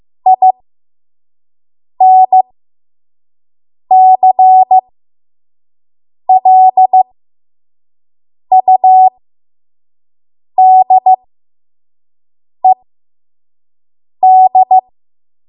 Morse code, heard as a keyed tone, about 15 wpm, spaced as for 5 wpm, 750 Hz. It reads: INCLUDED
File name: Morse / 05WPM